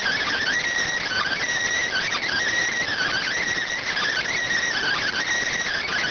Emergency Alert System (SAME) tones (header bursts), heard as harsh buzzing bursts. It is heard from 1.0 to 5.2 s.